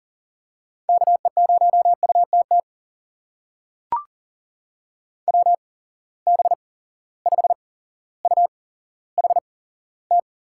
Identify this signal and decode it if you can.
Morse code, heard as a keyed tone, about 38 wpm, spaced as for 17 wpm, 700 Hz. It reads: KE0UTT WB5UHT
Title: USA Call Signs - 6 Characters - 8x Character Spacing - 40wpm